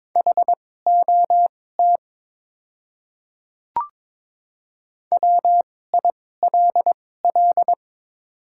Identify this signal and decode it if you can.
Morse code, heard as a keyed tone, about 22 wpm, spaced as for 16 wpm, 700 Hz. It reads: HOT WILL
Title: 2x Character Spacing - Top 100 Words - 22wpm